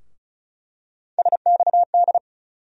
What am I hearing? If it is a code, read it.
Morse code, heard as a keyed tone, about 36 wpm, 700 Hz. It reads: SXD
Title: USA Call Sign Postfixes - 3 Letters 35wpm